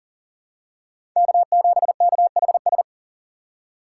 Morse code, heard as a keyed tone, about 38 wpm, 700 Hz. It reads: K7KHS
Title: USA Call Signs - 5 Characters - 4x Character Spacing - 40wpm